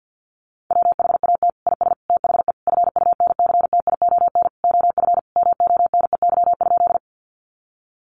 Morse code, heard as a keyed tone, about 50 wpm, 700 Hz. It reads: WHAT IS THE FUNCTION OF MONEY?
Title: Sentences from Top 1100 Words 50wpm